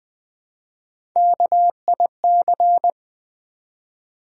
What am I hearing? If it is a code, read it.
Morse code, heard as a keyed tone, about 20 wpm, 700 Hz. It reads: KIC
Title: Build A Word - 1 Letter At A Time - 4-Letter Words - 20wpm